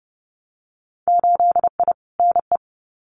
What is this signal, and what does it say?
Morse code, heard as a keyed tone, about 30 wpm, 700 Hz. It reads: 8I NE
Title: Winter Field Day Exchanges Encore 30wpm